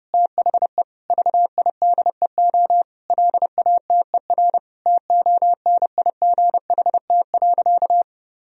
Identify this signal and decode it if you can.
Morse code, heard as a keyed tone, about 30 wpm, 700 Hz. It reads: THE VIDEO LATER TONIGHT.
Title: Sentences from Top 1000 Words - Review Part 4 - 30wpm